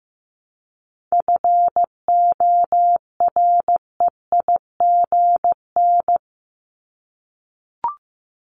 Morse code, heard as a keyed tone, about 15 wpm, 700 Hz. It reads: FOREIGN
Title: Top 1251-1500 Words 15wpm